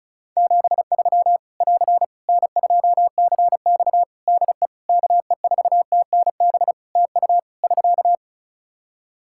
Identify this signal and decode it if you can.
Morse code, heard as a keyed tone, about 36 wpm, 700 Hz. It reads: 73 <AR> N2CX DE KE4TNB TU <SK>
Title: Practice QSO - Exchange 4 - Ending - 35wpm